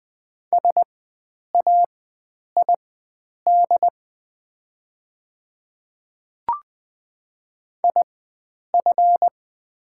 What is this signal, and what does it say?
Morse code, heard as a keyed tone, about 20 wpm, spaced as for 9 wpm, 700 Hz. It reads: SAID IF
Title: 4x Character Spacing - Top 100 Words - 20wpm